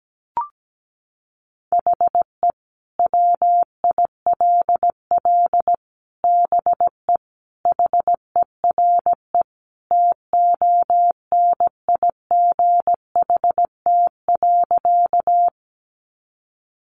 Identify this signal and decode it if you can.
Morse code, heard as a keyed tone, about 17 wpm, 700 Hz. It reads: HE WILL BE HERE TONIGHT.
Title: Sentences from Top 600 Words 17wpm